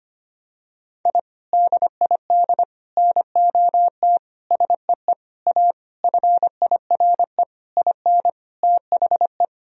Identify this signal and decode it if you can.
Morse code, heard as a keyed tone, about 25 wpm, 700 Hz. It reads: I DID NOT SEE A FIRE IN THE
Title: Sentences from Top 1000 Words - Review Part 2 - 25wpm